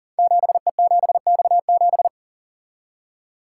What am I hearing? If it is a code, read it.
Morse code, heard as a keyed tone, about 39 wpm, 690 Hz. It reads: 7E7X7
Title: Code Groups - 5 Letter:Numbers - 40wpm